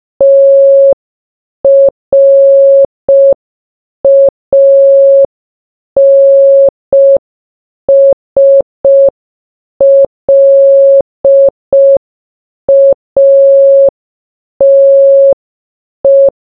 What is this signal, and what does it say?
Morse code, heard as a keyed tone, about 5 wpm, 550 Hz. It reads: TRANSLATE
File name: baba-is-you-the-folder-megapack-linux-macos-windows-gamerip-2024 / dobyxyjbym / 30. Morse